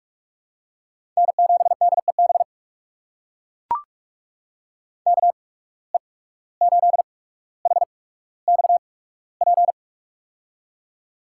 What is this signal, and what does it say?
Morse code, heard as a keyed tone, about 44 wpm, spaced as for 18 wpm, 700 Hz. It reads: N7DEB KE8HXP
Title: USA Call Signs - 8x Character Spacing - 45wpm